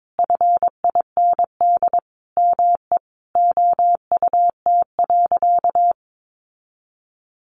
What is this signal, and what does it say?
Morse code, heard as a keyed tone, about 22 wpm, 700 Hz. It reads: FIND ME OUT.
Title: Sentences from Top 500 Words 22wpm